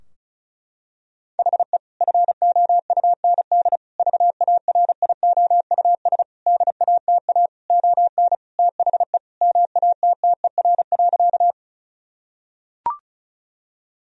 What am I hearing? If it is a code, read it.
Morse code, heard as a keyed tone, about 36 wpm, 700 Hz. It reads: HE FOUND VARIOUS DATA ON THE MATTER.
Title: Sentences from Top 1000 Words - Review Part 4 - 35wpm